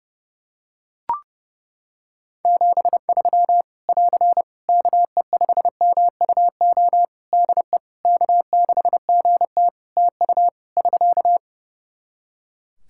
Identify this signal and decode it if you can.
Morse code, heard as a keyed tone, about 32 wpm, 700 Hz. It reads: 73 <AR> KE5MUO DE K6GT TU <SK>
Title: Practice QSO - Exchange 4 - Ending - 30wpm